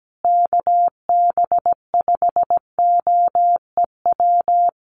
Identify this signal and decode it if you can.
Morse code, heard as a keyed tone, about 17 wpm, 700 Hz. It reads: KB5OEW
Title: USA Call Signs Repeated 3x 17wpm